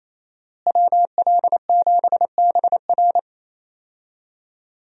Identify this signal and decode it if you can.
Morse code, heard as a keyed tone, about 27 wpm, 700 Hz. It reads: WL7BR
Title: USA Call Signs - 5 Characters - Repeated 3x - 28wpm